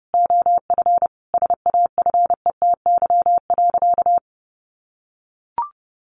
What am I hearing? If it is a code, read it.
Morse code, heard as a keyed tone, about 30 wpm, 700 Hz. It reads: OF SAFETY.
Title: Sentences from Top 1000 Words 30wpm